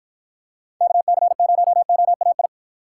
Morse code, heard as a keyed tone, about 52 wpm, 700 Hz. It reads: KC0OAI